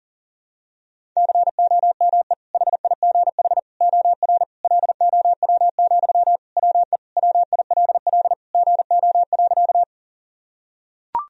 Morse code, heard as a keyed tone, about 40 wpm, 700 Hz. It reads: COME HIGH OR LOW, WE WILL GO.
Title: Sentences from Top 1000 Words - Review Part 1 - 40wpm